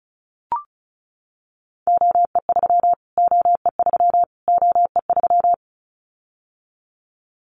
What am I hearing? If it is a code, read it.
Morse code, heard as a keyed tone, about 34 wpm, 700 Hz. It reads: OE3 OE3 OE3
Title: Callsign Prefixes for Top 25 DXCC - Repeated 3x - 35wpm